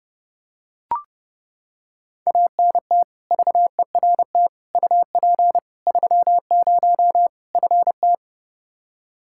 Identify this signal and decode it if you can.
Morse code, heard as a keyed tone, about 30 wpm, 700 Hz. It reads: ANT VERT UP 30 FT
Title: K1USN SST QSO Practice 30wpm